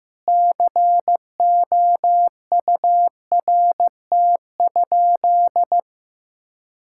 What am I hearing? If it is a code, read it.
Morse code, heard as a keyed tone, about 15 wpm, 700 Hz. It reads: COURT?
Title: Sentences from Top 1600 Words 15wpm